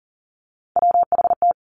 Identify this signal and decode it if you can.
Morse code, heard as a keyed tone, about 40 wpm, 700 Hz. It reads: WHT
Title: Three Challenging Letter-Number Combinations 40wpm